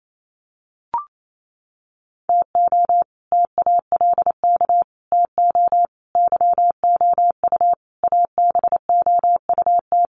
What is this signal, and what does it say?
Morse code, heard as a keyed tone, about 27 wpm, 700 Hz. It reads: TO TALK TO YOU ABOUT
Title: Sets of 5 Words Encore 28wpm